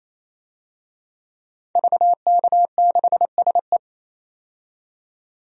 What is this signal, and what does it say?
Morse code, heard as a keyed tone, about 28 wpm, 700 Hz. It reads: VK6SE
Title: DXCC-100 Call Signs - 5 Characters - 28wpm